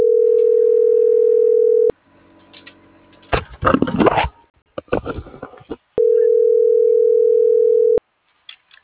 Ringback tone (North America), 440 + 480 Hz.
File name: CENTRAL FRALSE / out aidee.cruz